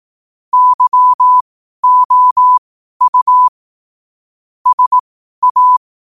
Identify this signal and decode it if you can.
Morse code, heard as a keyed tone, about 18 wpm, spaced as for 12 wpm, 1000 Hz. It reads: YOU SA
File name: qotdfiles / 12wpm